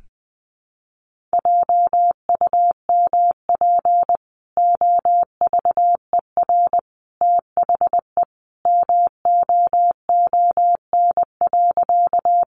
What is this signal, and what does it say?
Morse code, heard as a keyed tone, about 20 wpm, 700 Hz. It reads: JUMP OVER THE MOON.
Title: Sentences from Top 1300 Words 20wpm